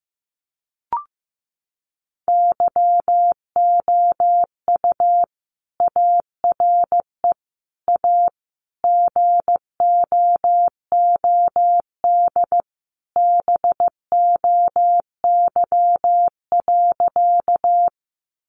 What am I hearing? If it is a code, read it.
Morse code, heard as a keyed tone, about 15 wpm, 700 Hz. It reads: YOU ARE A GOOD BOY.
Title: Sentences from Top 200 Words 15wpm